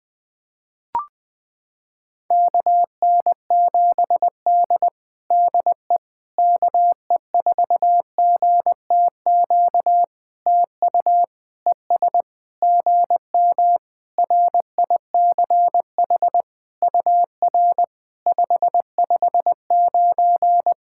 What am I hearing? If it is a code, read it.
Morse code, heard as a keyed tone, about 20 wpm, 700 Hz. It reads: KN7D DE KE4GTQ TU ES GM RICH UR 559
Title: POTA QSO Practice - 20wpm